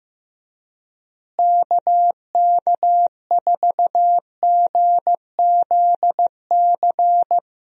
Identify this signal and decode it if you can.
Morse code, heard as a keyed tone, about 15 wpm, 700 Hz. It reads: KK4GZC
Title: USA Call Signs - 6 Characters - Repeated 2x - 15wpm